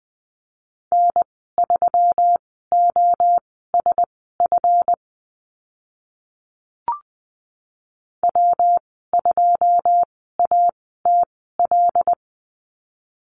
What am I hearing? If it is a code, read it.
Morse code, heard as a keyed tone, about 20 wpm, spaced as for 14 wpm, 700 Hz. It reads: N3OSF W2ATL
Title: 140 Lesson 23g - Letter f - Call Signs 20wpm